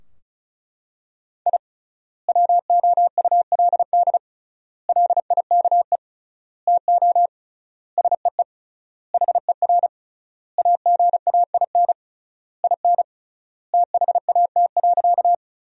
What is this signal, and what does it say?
Morse code, heard as a keyed tone, about 36 wpm, 700 Hz. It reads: I WOULD LIKE TO SEE HER AGAIN IN THAT.
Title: 3x Word-Spacing - Sentences from Top 200 Words - 35wpm